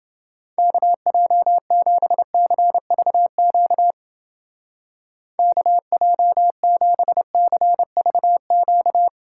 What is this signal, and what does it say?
Morse code, heard as a keyed tone, about 28 wpm, 700 Hz. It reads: KJ7CVQ KJ7CVQ